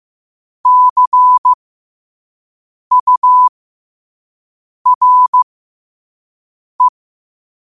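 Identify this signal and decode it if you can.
Morse code, heard as a keyed tone, about 15 wpm, spaced as for 5 wpm, 1000 Hz. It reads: CURE
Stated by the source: Morse code quote of the day sent at 5 wpm.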